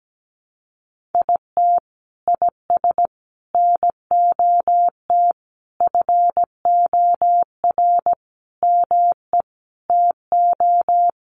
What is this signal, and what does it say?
Morse code, heard as a keyed tone, about 17 wpm, 700 Hz. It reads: IT IS NOT FOR ME TO
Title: Sentences from Top 1500 Words 17wpm